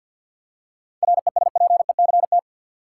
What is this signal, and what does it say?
Morse code, heard as a keyed tone, about 52 wpm, 700 Hz. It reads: RESPECT
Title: Seven Letter Words - 50wpm